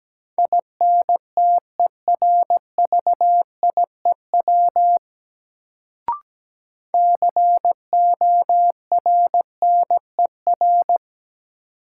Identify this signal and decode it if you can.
Morse code, heard as a keyed tone, about 17 wpm, 700 Hz. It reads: INTERVIEW CORNER
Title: Sentences from Top 1600 Words - New Word Drill - 17wpm